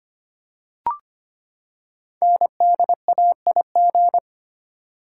Morse code, heard as a keyed tone, about 25 wpm, 700 Hz. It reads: NDAIG
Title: Code Groups - 5 Letters - 25wpm